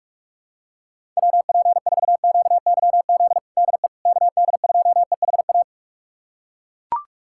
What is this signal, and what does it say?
Morse code, heard as a keyed tone, about 41 wpm, 700 Hz. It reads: WP4QYZ DE KD2EHA
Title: USA Callsign DE Callsign - 6 Characters - 45wpm